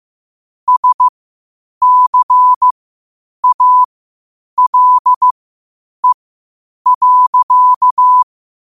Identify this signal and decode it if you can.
Morse code, heard as a keyed tone, about 15 wpm, spaced as for 8 wpm, 1000 Hz. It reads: SCALE.
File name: qotdfiles / 8wpm